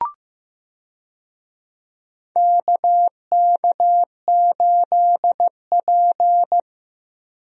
Morse code, heard as a keyed tone, about 15 wpm, 700 Hz. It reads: KK8P
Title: USA Call Signs - 4 Characters - 15wpm